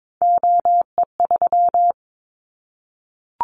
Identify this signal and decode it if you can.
Morse code, heard as a keyed tone, about 22 wpm, 700 Hz. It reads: OE3
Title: Callsign Prefixes for Top 100 DXCC - 22wpm